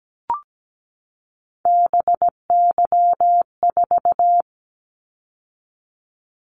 Morse code, heard as a keyed tone, about 17 wpm, 700 Hz. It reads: BY4
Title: Callsign Prefixes for Top 100 DXCC - 17wpm